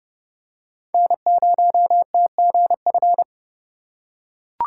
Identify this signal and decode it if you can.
Morse code, heard as a keyed tone, about 30 wpm, 700 Hz. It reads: N0TGF
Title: USA Call Signs - 5 Characters - 3x Character Spacing - 30wpm